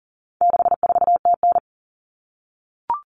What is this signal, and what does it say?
Morse code, heard as a keyed tone, about 40 wpm, 700 Hz. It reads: 64TN
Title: Common Cut Numbers - 4-Digit 40wpm